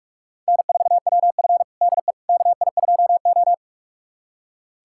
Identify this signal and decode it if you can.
Morse code, heard as a keyed tone, about 43 wpm, 700 Hz. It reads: N4WF DE KI2O
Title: USA Callsign DE Callsign - 4 Characters - 45wpm